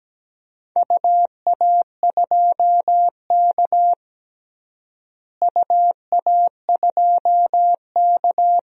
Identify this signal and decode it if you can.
Morse code, heard as a keyed tone, about 17 wpm, 700 Hz. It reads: UA2K UA2K
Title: DXCC-100 Call Signs - 4 Characters - Repeated 3x - 17wpm